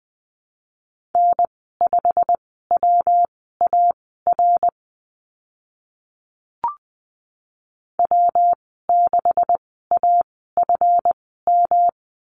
Morse code, heard as a keyed tone, about 20 wpm, spaced as for 15 wpm, 700 Hz. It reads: N5WAR W6AFM
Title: 147 Lesson 24g - Letter y - Call Signs 20wpm